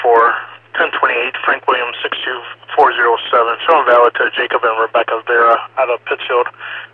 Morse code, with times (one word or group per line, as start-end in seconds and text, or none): none